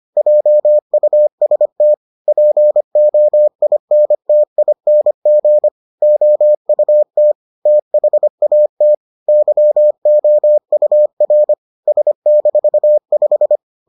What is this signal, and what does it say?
Morse code, heard as a keyed tone, about 25 wpm, 600 Hz. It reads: JUST POINTING OUT THAT YOUR S-5